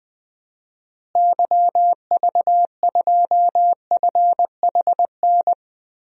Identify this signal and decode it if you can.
Morse code, heard as a keyed tone, about 20 wpm, 700 Hz. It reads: YV2FHN
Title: DXCC-100 Call Signs - 6 Characters - Repeated 3x - 20wpm